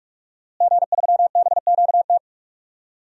Morse code, heard as a keyed tone, about 46 wpm, 700 Hz. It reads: G3BQT
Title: DXCC-100 Call Signs - 5 Characters - Repeated 3x - 45wpm